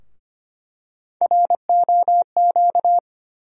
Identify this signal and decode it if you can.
Morse code, heard as a keyed tone, about 25 wpm, 700 Hz. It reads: ROQ
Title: USA Call Sign Postfixes - 3 Letters 25wpm